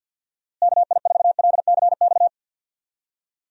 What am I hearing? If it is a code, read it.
Morse code, heard as a keyed tone, about 46 wpm, 700 Hz. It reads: KI4LCX
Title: USA Call Signs - 6 Characters - Repeated 2x - 50wpm